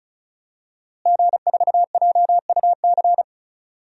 Morse code, heard as a keyed tone, about 36 wpm, 700 Hz. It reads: G4JUC
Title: DXCC-100 Call Signs - 5 Characters - Repeated 3x - 35wpm